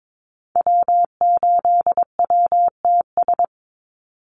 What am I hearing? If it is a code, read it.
Morse code, heard as a keyed tone, about 22 wpm, 700 Hz. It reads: W8WTS
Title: Call Signs Repeated 2x - 22wpm